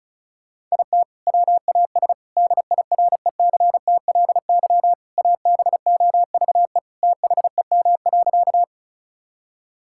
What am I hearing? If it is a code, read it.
Morse code, heard as a keyed tone, about 34 wpm, 700 Hz. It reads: IT WAS DIRECTLY ABOVE THEM.
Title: Sentences from Top 1000 Words - Review Part 5 - 35wpm